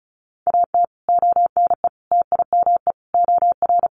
Morse code, heard as a keyed tone, about 34 wpm, 700 Hz. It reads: AT ONE TIME OR
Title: Sets of 5 Words Encore 35wpm